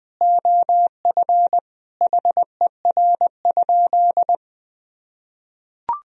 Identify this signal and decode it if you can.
Morse code, heard as a keyed tone, about 21 wpm, 700 Hz. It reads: OF HER?
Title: Sentences from Top 1000 Words - Review Part 2 - 20wpm